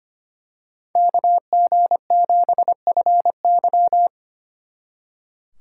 Morse code, heard as a keyed tone, about 28 wpm, 700 Hz. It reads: KG7FY
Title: Speed Racing - USA 2x2 Call Signs 25wpm